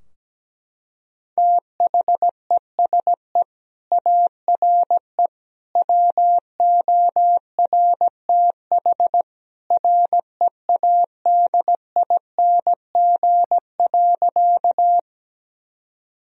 Morse code, heard as a keyed tone, about 17 wpm, 700 Hz. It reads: THESE ARE WORTH READING.
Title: Sentences from Top 1000 Words - Review Part 3 - 17wpm